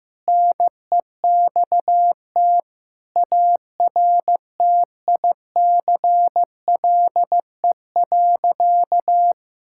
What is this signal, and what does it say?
Morse code, heard as a keyed tone, about 15 wpm, 700 Hz. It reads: NEXT ARTICLE.
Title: Sentences from Top 1000 Words - Review Part 5 - 15wpm